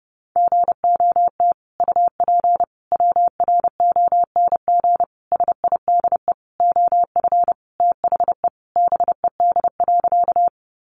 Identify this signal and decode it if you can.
Morse code, heard as a keyed tone, about 30 wpm, 700 Hz. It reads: GOT UP WRONG SIDE OF THE BED.
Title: Sentences from Top 900 Words 30wpm